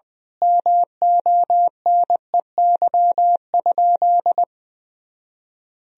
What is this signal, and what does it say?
Morse code, heard as a keyed tone, about 20 wpm, 700 Hz. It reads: MONEY?
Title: Sentences from Top 400 Words and Abbreviations 20wpm